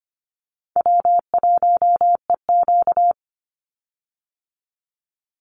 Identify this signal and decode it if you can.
Morse code, heard as a keyed tone, about 25 wpm, 700 Hz. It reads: W1EQ
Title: Call Signs - 25wpm